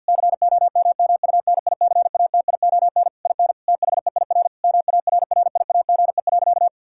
Morse code, heard as a keyed tone, about 48 wpm, 700 Hz. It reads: COMMUNICATION IN THEIR MARRIAGE.
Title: Sentences from Top 1000 Words - Review Part 6 - 50wpm